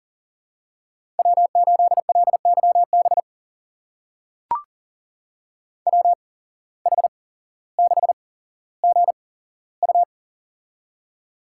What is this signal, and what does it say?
Morse code, heard as a keyed tone, about 39 wpm, spaced as for 17 wpm, 700 Hz. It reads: W8LYB WH6GU
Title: USA Call Signs - 5 Characters - 8x Character Spacing - 40wpm